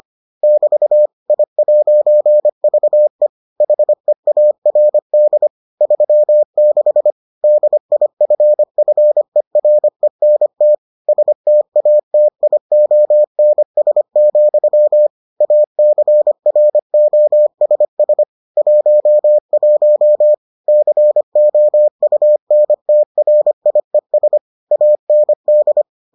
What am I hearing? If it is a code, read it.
Morse code, heard as a keyed tone, about 25 wpm, 600 Hz. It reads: <BT> I'VE HEARD 36 DIFFERENT STATIONS, ACROSS 11 COUNTRIES AND